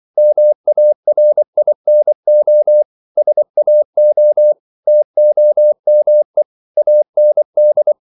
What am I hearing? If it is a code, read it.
Morse code, heard as a keyed tone, about 24 wpm, 600 Hz. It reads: MARINO SAO TOME AND
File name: CW / Practice / NationalCapitals24